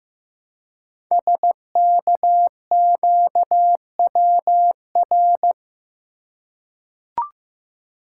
Morse code, heard as a keyed tone, about 15 wpm, 700 Hz. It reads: SKQWR
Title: Code Groups - 5 Letter:Numbers - 15wpm